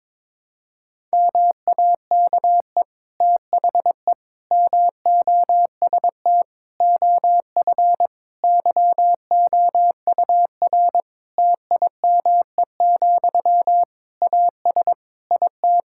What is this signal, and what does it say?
Morse code, heard as a keyed tone, about 22 wpm, 700 Hz. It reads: MAKE THE MOST OF YOUR TIME, AS IT
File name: Difficult Sentences with Hints - 22wpm